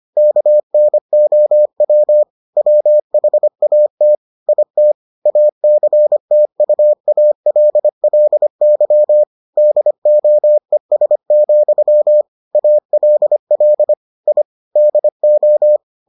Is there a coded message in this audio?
Yes: Morse code from 0.2 to 15.8 s, heard as a keyed tone, about 25 wpm, 600 Hz. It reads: KNOW WHAT IT ACTUALLY DOES, ALL I DO